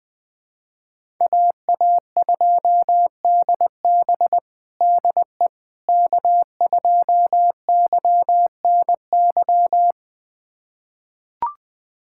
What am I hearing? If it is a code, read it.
Morse code, heard as a keyed tone, about 20 wpm, 700 Hz. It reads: AA2DB DE K2YNY
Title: USA Callsign DE Callsign - 5 Characters - 20wpm